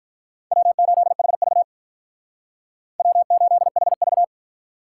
Morse code, heard as a keyed tone, about 45 wpm, 700 Hz. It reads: W8HV W8HV